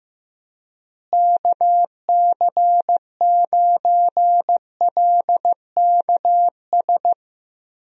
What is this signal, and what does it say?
Morse code, heard as a keyed tone, about 15 wpm, 700 Hz. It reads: KC9LKS
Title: USA Call Signs - 6 Characters - Repeated 3x - 15wpm